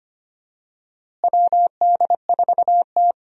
Morse code, heard as a keyed tone, about 25 wpm, 700 Hz. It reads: WD4T
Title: USA Call Signs - 4 Characters - Repeated 3x - 25wpm